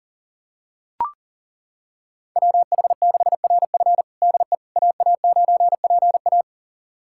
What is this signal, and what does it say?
Morse code, heard as a keyed tone, about 40 wpm, 700 Hz. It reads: WH6RF DE AA9PA
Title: USA Callsign DE Callsign - 5 Characters - 40wpm